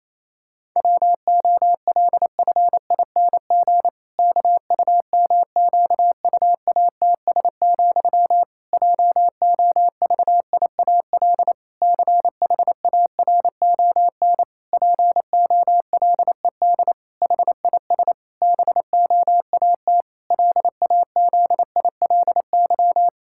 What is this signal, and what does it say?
Morse code, heard as a keyed tone, about 28 wpm, 700 Hz. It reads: WOLFING KUMQUATS, JOVIAL CHARON POLED HIS BOAT LAZILY
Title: Pangram Challenge v2 - 28wpm